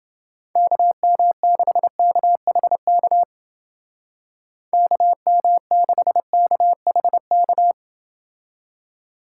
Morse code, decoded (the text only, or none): KM6KHK KM6KHK